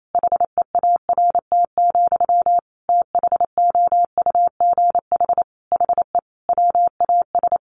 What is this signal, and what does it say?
Morse code, heard as a keyed tone, about 28 wpm, 700 Hz. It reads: HEART, THOUGH HE WAS
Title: Sentences from Top 1400 Words 28wpm